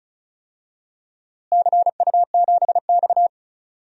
Morse code, heard as a keyed tone, about 36 wpm, 700 Hz. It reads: CU7X
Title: DXCC-100 Call Signs - 4 Characters - Repeated 2x - 35wpm